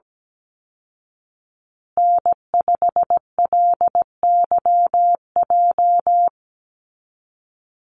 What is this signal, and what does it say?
Morse code, heard as a keyed tone, about 17 wpm, 700 Hz. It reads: N5LYJ
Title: Call Signs Repeated 2x - 17wpm